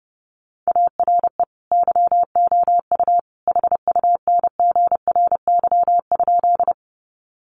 Morse code, heard as a keyed tone, about 30 wpm, 700 Hz. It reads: ARE YOU HUNGRY?
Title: Sentences from Top 1400 Words 30wpm